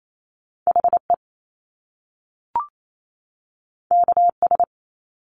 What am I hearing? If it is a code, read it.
Morse code, heard as a keyed tone, about 28 wpm, 700 Hz. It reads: H E K S
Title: Top Two Letter Combinations in Words 28wpm